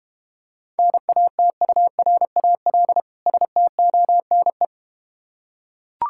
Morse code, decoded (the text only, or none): NATURAL STONE